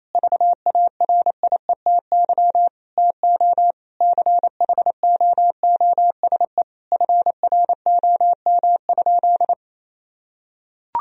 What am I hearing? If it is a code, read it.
Morse code, heard as a keyed tone, about 28 wpm, 700 Hz. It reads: VARIETY TO CHOOSE FROM?
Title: Sentences from Top 1000 Words - Review Part 4 - 28wpm